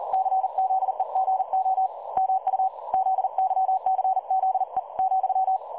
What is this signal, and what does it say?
Morse code, heard as a keyed tone, about 40 wpm, 760 Hz. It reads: CZEGO MU ZYCZE ,
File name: dlazaawansowanych / nagraniazpasma / websdr recording 3565.0kHz